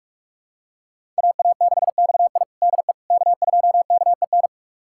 Morse code, heard as a keyed tone, about 47 wpm, 700 Hz. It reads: AA6XI DE K2KEN
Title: USA Callsign DE Callsign - 5 Characters - 45wpm